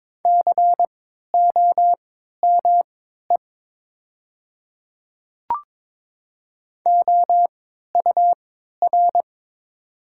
Morse code, heard as a keyed tone, about 22 wpm, spaced as for 12 wpm, 700 Hz. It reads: COME OUR
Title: 3x Character Spacing - Top 500 Words - 22wpm